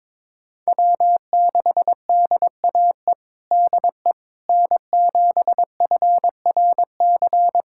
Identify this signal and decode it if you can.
Morse code, heard as a keyed tone, about 22 wpm, 700 Hz. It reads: W6DAE DE N7FRC
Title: USA Callsign DE Callsign - 5 Characters - 22wpm